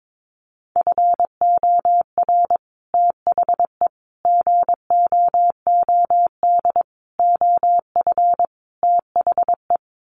Morse code, heard as a keyed tone, about 22 wpm, 700 Hz. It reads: FOR THE GOOD OF THE
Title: Sets of 5 Words Encore2 22wpm